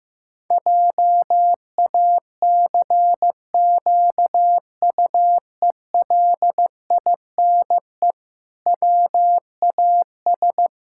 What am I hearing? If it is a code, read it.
Morse code, heard as a keyed tone, about 15 wpm, 700 Hz. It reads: JACQUELINE WAS
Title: Pangram Challenge v2 - 15wpm